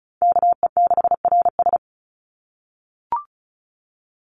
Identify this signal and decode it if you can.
Morse code, heard as a keyed tone, about 34 wpm, 700 Hz. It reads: KE6RS
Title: NA SOTA Chaser Callsigns 35wpm